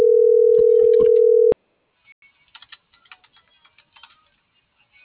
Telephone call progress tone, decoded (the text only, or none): ringback tone